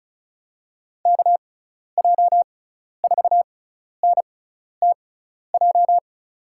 Morse code, heard as a keyed tone, about 35 wpm, spaced as for 12 wpm, 700 Hz. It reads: KJ4NTJ
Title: USA Call Signs - 6 Characters - 6x Character Spacing - 35wpm